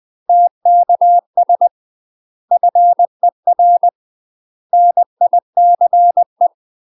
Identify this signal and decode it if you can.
Morse code, heard as a keyed tone, about 20 wpm, 700 Hz. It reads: TKS FER NICE
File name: UK MASTER QSOs TRAINING / 20WPM / UK MASTER QSOs 0043